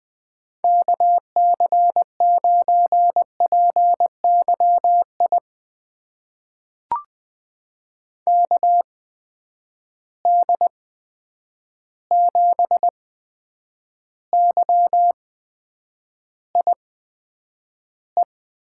Morse code, heard as a keyed tone, about 20 wpm, spaced as for 9 wpm, 700 Hz. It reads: KC9PYI KD7YIE
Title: USA Call Signs - 6 Characters - 8x Character Spacing - 20wpm